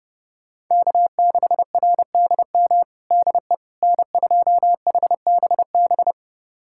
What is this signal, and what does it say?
Morse code, heard as a keyed tone, about 30 wpm, 700 Hz. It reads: K6RDM DE N2HBB
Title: USA Callsign DE Callsign - 5 Characters - 30wpm